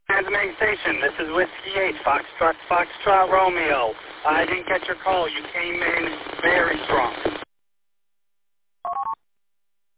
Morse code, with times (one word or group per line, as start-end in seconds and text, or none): none